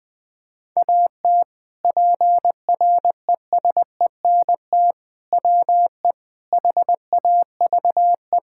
Morse code, heard as a keyed tone, about 20 wpm, 700 Hz. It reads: AT PRESENT WE HAVE
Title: Sentences from Top 1000 Words - Review Part 5 - 20wpm